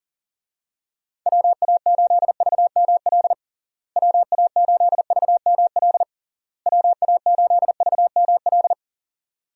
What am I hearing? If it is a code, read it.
Morse code, heard as a keyed tone, about 37 wpm, 700 Hz. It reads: WA8VML WA8VML WA8VML
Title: USA Call Signs - 6 Characters - Repeated 3x - 40wpm